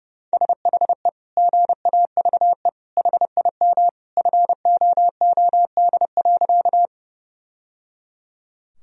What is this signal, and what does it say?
Morse code, heard as a keyed tone, about 30 wpm, 700 Hz. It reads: SHE GAVE HIM FOOD.
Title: Sentences from Top 1000 Words - Review Part 1 - 30wpm